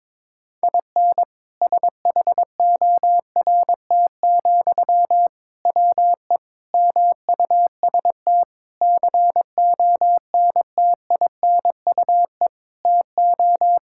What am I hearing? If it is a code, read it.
Morse code, heard as a keyed tone, about 22 wpm, 700 Hz. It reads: IN SHORT, WE MUST CONTINUE TO
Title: Sentences from Top 1600 Words 22wpm